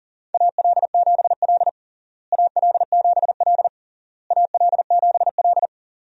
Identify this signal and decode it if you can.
Morse code, heard as a keyed tone, about 37 wpm, 700 Hz. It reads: AL7L AL7L AL7L
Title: USA Call Signs - 4 Characters - Repeated 3x - 40wpm